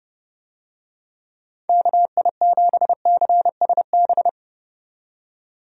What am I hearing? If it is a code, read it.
Morse code, heard as a keyed tone, about 30 wpm, 700 Hz. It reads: KI7CSB
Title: USA Call Signs - 6 Characters - 30wpm